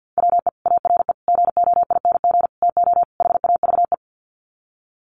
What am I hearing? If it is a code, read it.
Morse code, heard as a keyed tone, about 48 wpm, 700 Hz. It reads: WE ARE GOING TO HAVE
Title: Sets of 5 Words Encore 50wpm